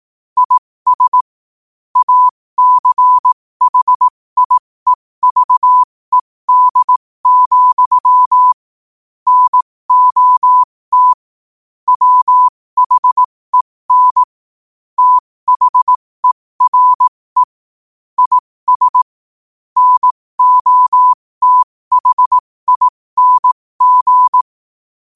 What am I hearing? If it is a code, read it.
Morse code, heard as a keyed tone, about 18 wpm, spaced as for 15 wpm, 1000 Hz. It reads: IS ACHIEVED, NOT WHEN THERE IS NOTHING
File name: qotdfiles / 15wpm